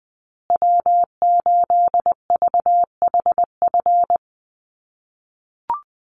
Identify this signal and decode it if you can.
Morse code, heard as a keyed tone, about 20 wpm, 700 Hz. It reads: W8VHF
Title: USA Call Signs Repeated 2x 20wpm